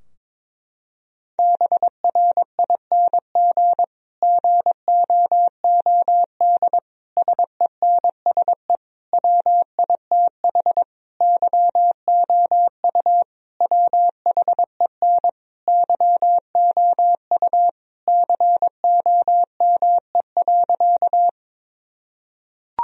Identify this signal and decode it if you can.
Morse code, heard as a keyed tone, about 22 wpm, 700 Hz. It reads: BRING GOOD SENSE WITH YOU WHEN YOU COME.
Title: Sentences from Top 1000 Words - Review Part 4 - 22wpm